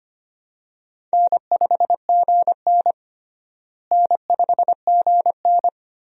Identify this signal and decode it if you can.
Morse code, heard as a keyed tone, about 25 wpm, 700 Hz. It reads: N5GN N5GN
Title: USA Call Signs - 4 Characters - Repeated 3x - 25wpm